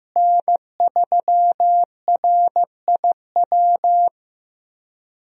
Morse code, heard as a keyed tone, about 15 wpm, 700 Hz. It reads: N3RIW
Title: USA Call Signs - 5 Characters - Repeated 3x - 15wpm